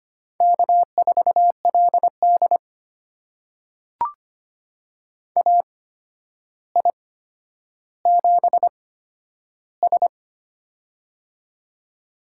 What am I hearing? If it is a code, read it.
Morse code, heard as a keyed tone, about 25 wpm, spaced as for 11 wpm, 700 Hz. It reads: K4LD AI7S
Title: USA Call Signs - 4 Characters - 8x Character Spacing - 25wpm